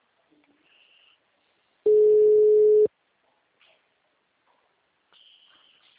Ringback tone (Europe, 425 Hz).
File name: out- -2003- - - .54334